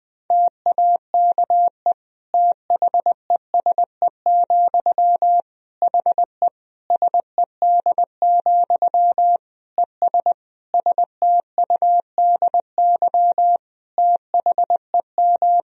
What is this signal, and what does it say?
Morse code, heard as a keyed tone, about 21 wpm, 700 Hz. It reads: TAKE THESE, HE SED, ES STUDY THEM
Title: Sentences from Top 300 Words and Abbreviations 20wpm